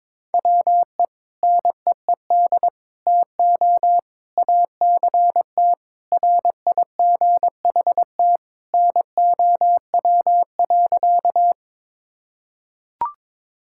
Morse code, heard as a keyed tone, about 22 wpm, 700 Hz. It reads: WE NEED TO ACT RIGHT NOW.
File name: practice / 22wpm / Sentences with 2-Word Phrases - 22wpm